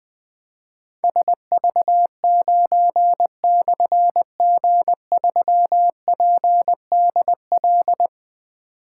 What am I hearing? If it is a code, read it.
Morse code, heard as a keyed tone, about 20 wpm, 700 Hz. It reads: SV9/G3PDL
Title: Difficult Call Signs 20wpm